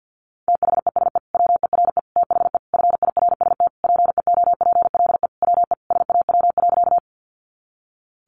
Morse code, heard as a keyed tone, about 50 wpm, 700 Hz. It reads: THESE WERE THE FIRST PEOPLE WE SAW.
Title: Sentences from Top 300 Words 50wpm